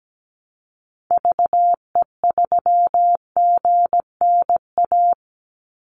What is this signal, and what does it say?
Morse code, heard as a keyed tone, about 17 wpm, 700 Hz. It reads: VE3GNA
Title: Call Signs - 17wpm